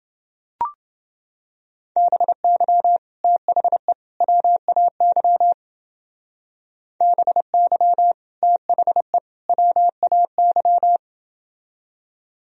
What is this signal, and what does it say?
Morse code, heard as a keyed tone, about 30 wpm, 700 Hz. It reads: BY THE WAY BY THE WAY